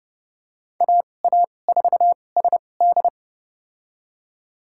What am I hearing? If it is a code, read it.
Morse code, heard as a keyed tone, about 30 wpm, spaced as for 22 wpm, 700 Hz. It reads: AA4SD
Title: USA Call Signs - 5 Characters - 2x Character Spacing - 30wpm